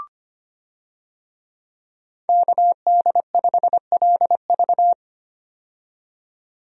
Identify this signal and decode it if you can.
Morse code, heard as a keyed tone, about 25 wpm, 690 Hz. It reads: KD5LV
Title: USA Call Signs - 5 Characters - 25wpm